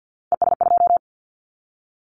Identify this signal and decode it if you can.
Morse code, heard as a keyed tone, about 48 wpm, 700 Hz. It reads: ES2
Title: Callsign Prefixes for Top 100 DXCC - 50wpm